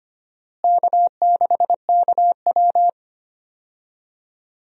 Morse code, heard as a keyed tone, about 25 wpm, 700 Hz. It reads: K6KW
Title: USA Call Signs - 4 Characters - Repeated 2x - 25wpm